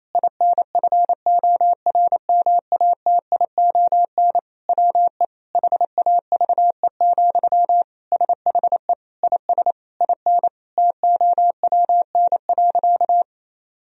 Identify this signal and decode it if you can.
Morse code, heard as a keyed tone, about 28 wpm, 700 Hz. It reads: INFORMATION WE HAVE, SHE IS IN TOWN.
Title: Sentences from Top 1000 Words - Review Part 4 - 28wpm